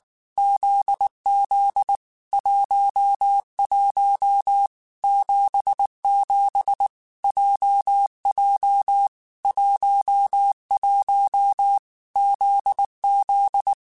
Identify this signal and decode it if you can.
Morse code, heard as a keyed tone, about 19 wpm, 800 Hz. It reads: ZZ 11 77 JJ 11 ZZ
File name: Basic Sets / Basic Letter Groups Collective7 / JZ17@18